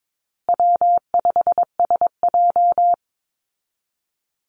Morse code, heard as a keyed tone, about 22 wpm, 700 Hz. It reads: W5SJ
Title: Call Signs Repeated 3x - 22wpm